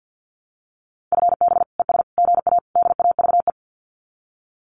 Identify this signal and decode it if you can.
Morse code, heard as a keyed tone, about 50 wpm, 700 Hz. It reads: FB ES GA DAVE
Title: Common QSO Phrases Encore 50wpm